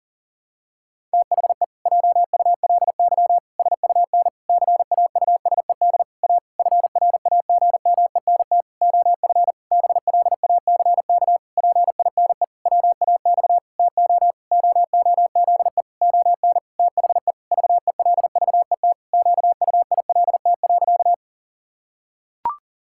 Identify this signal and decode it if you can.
Morse code, heard as a keyed tone, about 37 wpm, 700 Hz. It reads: THE JULY SUN CAUSED A FRAGMENT OF BLACK PINE WAX TO OOZE ON THE VELVET QUILT.
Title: Pangram Challenge v2 - 40wpm